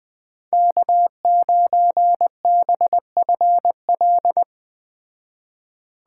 Morse code, heard as a keyed tone, about 20 wpm, 700 Hz. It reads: K9BFL
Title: Speed Racing - USA 1x3 Call Signs 17wpm